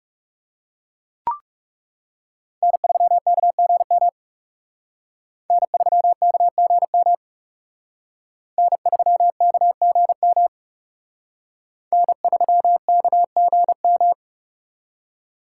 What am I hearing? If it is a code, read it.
Morse code, heard as a keyed tone, about 36 wpm, 700 Hz. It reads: N3KGM N3KGM N3KGM N3KGM
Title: Speed Racing - USA 1x3 Call Signs 30wpm